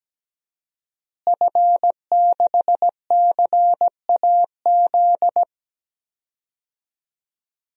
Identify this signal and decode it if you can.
Morse code, heard as a keyed tone, about 17 wpm, 700 Hz. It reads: F6CAZ
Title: DXCC-100 Call Signs - 5 Characters - Repeated 3x - 17wpm